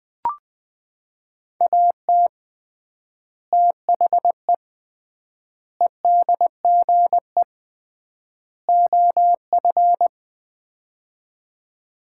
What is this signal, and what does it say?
Morse code, heard as a keyed tone, about 20 wpm, 700 Hz. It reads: AT THE EDGE OF
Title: 3x Word-Spacing - Sets of 4 Words - 20wpm